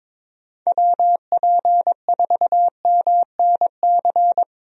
Morse code, heard as a keyed tone, about 22 wpm, 700 Hz. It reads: WP4MNC
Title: USA Call Signs - 6 Characters - Repeated 2x - 22wpm